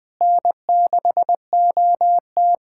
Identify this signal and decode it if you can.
Morse code, heard as a keyed tone, about 20 wpm, 700 Hz. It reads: N6OT
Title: USA Callsign DE Callsign - 4 Characters - 20wpm